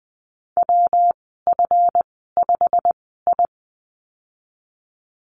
Morse code, heard as a keyed tone, about 19 wpm, spaced as for 14 wpm, 700 Hz. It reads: WF5I
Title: 173 Lesson 28g - Number 9 - Call Signs 20wpm